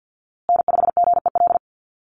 Morse code, heard as a keyed tone, about 50 wpm, 700 Hz. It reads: N5GEL
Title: USA Call Signs Repeated 2x 50wpm